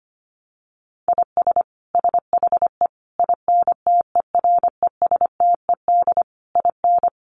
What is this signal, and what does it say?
Morse code, heard as a keyed tone, about 25 wpm, 700 Hz. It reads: IS SHE INTERESTED IN
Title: Sentences from Top 900 Words 25wpm